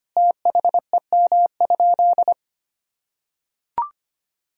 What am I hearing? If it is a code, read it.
Morse code, heard as a keyed tone, about 25 wpm, 700 Hz. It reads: THEM?
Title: Sentences from Top 1000 Words - Review Part 4 - 25wpm